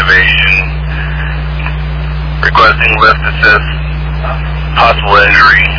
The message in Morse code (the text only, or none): none